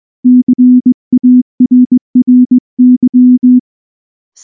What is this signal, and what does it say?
Morse code, heard as a keyed tone, about 20 wpm, 250 Hz. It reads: CARRY
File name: morse / morse-training-20WPM-2